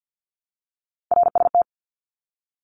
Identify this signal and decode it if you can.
Morse code, heard as a keyed tone, about 50 wpm, 700 Hz. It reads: RST
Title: Abbreviations 50wpm